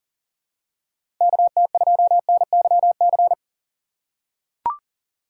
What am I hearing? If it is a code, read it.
Morse code, heard as a keyed tone, about 40 wpm, 700 Hz. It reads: KT2NYC
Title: USA Call Signs - 6 Characters - 4x Character Spacing - 40wpm